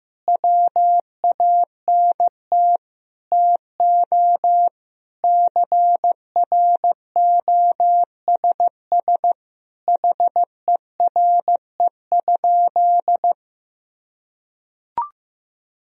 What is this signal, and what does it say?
Morse code, heard as a keyed tone, about 15 wpm, 700 Hz. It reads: WANT TO CROSS HERE?
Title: Sentences from Top 1000 Words - Review Part 1 - 15wpm